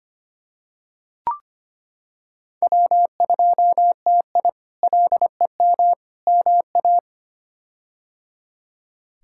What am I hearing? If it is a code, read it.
Morse code, heard as a keyed tone, about 25 wpm, 700 Hz. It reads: W2TI LEM MA
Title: K1USN SST QSO Practice - 25wpm